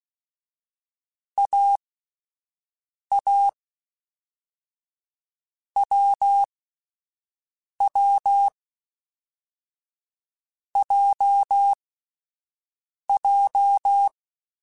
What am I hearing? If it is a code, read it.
Morse code, heard as a keyed tone, about 16 wpm, 800 Hz. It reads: A A W W J J
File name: Difficult Combos / Difficult Combos-mp3s / awj1 ndb6 gz7 @10